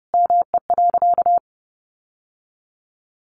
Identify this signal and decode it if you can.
Morse code, heard as a keyed tone, about 30 wpm, 700 Hz. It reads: ME.
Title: Sentences from Top 1200 Words 30wpm